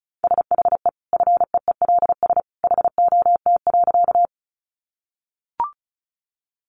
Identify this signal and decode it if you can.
Morse code, heard as a keyed tone, about 34 wpm, 700 Hz. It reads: SHE FEELS HOT.
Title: Easy Phrases 35wpm